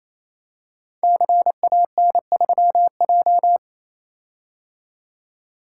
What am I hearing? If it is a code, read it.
Morse code, heard as a keyed tone, about 27 wpm, 700 Hz. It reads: CAN3J
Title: Code Groups - 5 Letter:Numbers - 28wpm